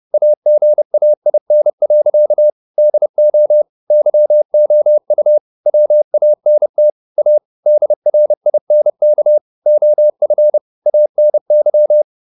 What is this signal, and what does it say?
Morse code, heard as a keyed tone, about 30 wpm, 600 Hz. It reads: AGAIN. DO YOU WANT A DRINK OF ANY